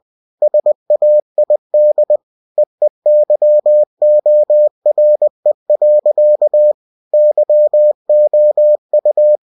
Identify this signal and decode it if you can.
Morse code, heard as a keyed tone, about 20 wpm, 600 Hz. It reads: SAID EEYORE. YOU